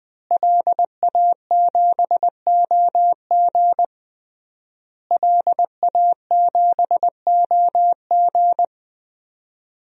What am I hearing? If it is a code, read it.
Morse code, heard as a keyed tone, about 20 wpm, 700 Hz. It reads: LA7OG LA7OG
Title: DXCC-100 Call Signs - 5 Characters - Repeated 3x - 20wpm